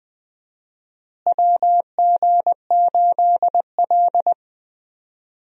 Morse code, heard as a keyed tone, about 20 wpm, 700 Hz. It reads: WG8L
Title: USA Call Signs - 4 Characters - Repeated 3x - 20wpm